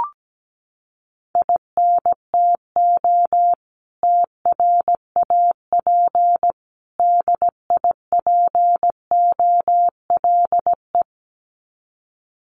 Morse code, heard as a keyed tone, about 17 wpm, 700 Hz. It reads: INTO TRAP DIPOLE
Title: Common QSO Phrases 17wpm